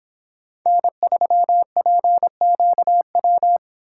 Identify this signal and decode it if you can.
Morse code, heard as a keyed tone, about 26 wpm, 700 Hz. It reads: N3PQW